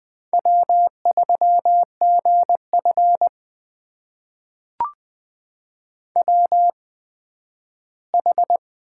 Morse code, heard as a keyed tone, about 20 wpm, spaced as for 12 wpm, 700 Hz. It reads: W3GF WH
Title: USA Call Signs - 8x Character Spacing - 20wpm